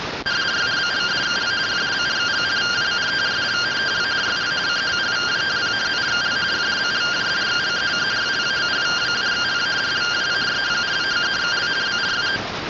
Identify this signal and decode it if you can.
Two-tone teleprinter signal, about 170 Hz shift.